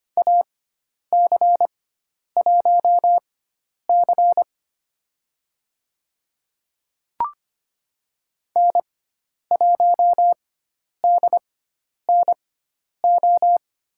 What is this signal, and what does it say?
Morse code, heard as a keyed tone, about 25 wpm, spaced as for 10 wpm, 700 Hz. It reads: AC1C N1DNO
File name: Lesson 16g - Letter c - Call Signs 25wpm